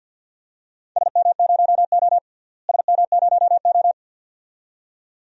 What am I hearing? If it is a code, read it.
Morse code, heard as a keyed tone, about 47 wpm, 700 Hz. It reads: SM0O SM0O
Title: DXCC-100 Call Signs - 4 Characters - Repeated 2x - 50wpm